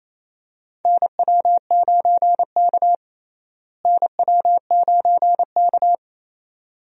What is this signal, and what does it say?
Morse code, heard as a keyed tone, about 27 wpm, 700 Hz. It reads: NW9K NW9K
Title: USA Call Signs - 4 Characters - Repeated 2x - 28wpm